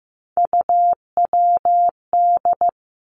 Morse code, heard as a keyed tone, about 15 wpm, 700 Hz. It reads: UWD
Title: Three Challenging Letter-Number Combinations 15wpm